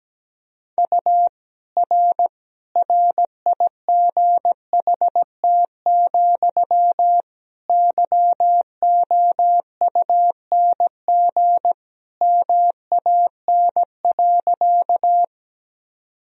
Morse code, heard as a keyed tone, about 17 wpm, 700 Hz. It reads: U R RIGHT, YOUNG MAN.
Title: Sentences from Top 400 Words and Abbreviations 17wpm